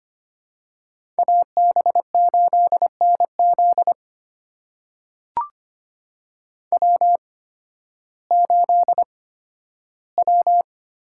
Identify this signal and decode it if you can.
Morse code, heard as a keyed tone, about 25 wpm, spaced as for 13 wpm, 700 Hz. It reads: AB8NZ W8W
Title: USA Call Signs - 5 Characters - 8x Character Spacing - 25wpm